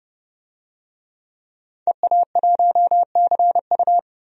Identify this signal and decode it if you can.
Morse code, heard as a keyed tone, about 32 wpm, 700 Hz. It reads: EA1CU
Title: DXCC-100 Call Signs - 5 Characters - 30wpm